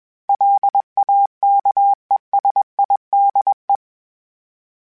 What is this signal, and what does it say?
Morse code, heard as a keyed tone, about 21 wpm, 800 Hz. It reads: LAKESIDE
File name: fromSMs / FromSMs mp3s / Cities@20